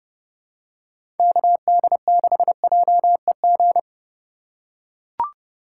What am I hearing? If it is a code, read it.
Morse code, heard as a keyed tone, about 30 wpm, 700 Hz. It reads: KD6JEG
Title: USA Call Signs - 2x Character Spacing - 30wpm